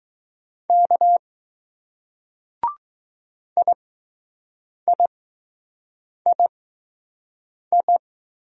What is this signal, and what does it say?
Morse code, heard as a keyed tone, about 20 wpm, spaced as for 6 wpm, 700 Hz. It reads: K IIII